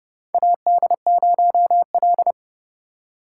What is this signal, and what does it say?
Morse code, heard as a keyed tone, about 29 wpm, 700 Hz. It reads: AD0L
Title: USA Call Signs - 4 Characters - 3x Character Spacing - 30wpm